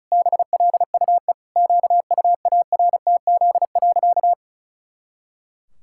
Morse code, heard as a keyed tone, about 34 wpm, 700 Hz. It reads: BLUE QUARTZ.
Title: Pangram Challenge v2 - 35wpm